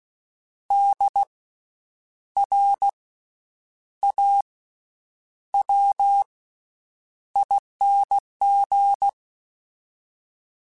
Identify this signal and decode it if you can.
Morse code, heard as a keyed tone, about 16 wpm, 800 Hz. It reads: D R A W ING
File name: Special INGs-mp3s / Special INGs-mp3s / Slow ing @15